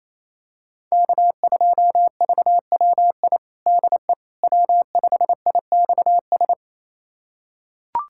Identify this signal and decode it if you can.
Morse code, heard as a keyed tone, about 28 wpm, 700 Hz. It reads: K2VWI DE W5IXS
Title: USA Callsign DE Callsign - 5 Characters - 28wpm